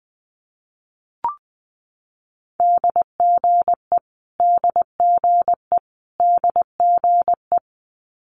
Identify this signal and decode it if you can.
Morse code, heard as a keyed tone, about 20 wpm, 700 Hz. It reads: DGE DGE DGE
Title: Multi-letter Phonemes Repeated 3x 20wpm